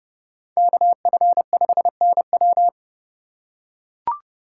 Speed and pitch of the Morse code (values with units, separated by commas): 29 wpm, 700 Hz